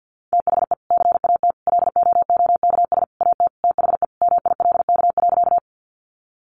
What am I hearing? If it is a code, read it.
Morse code, heard as a keyed tone, about 50 wpm, 700 Hz. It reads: THE CAT LOOKS AT THE MILK.
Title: Additional Easy Phrases 50wpm